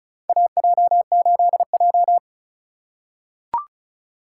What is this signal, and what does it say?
Morse code, heard as a keyed tone, about 36 wpm, 700 Hz. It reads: AJ8J
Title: USA Call Signs - 4 Characters - 3x Character Spacing - 35wpm